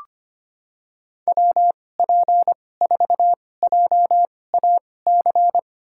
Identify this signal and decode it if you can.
Morse code, heard as a keyed tone, about 25 wpm, spaced as for 18 wpm, 700 Hz. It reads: WP4JAC
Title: USA Call Signs - 6 Characters - 2x Character Spacing - 25wpm